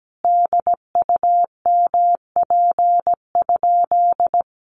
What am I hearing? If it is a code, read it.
Morse code, heard as a keyed tone, about 17 wpm, 700 Hz. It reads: DUMP?
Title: Sentences from Top 1300 Words 17wpm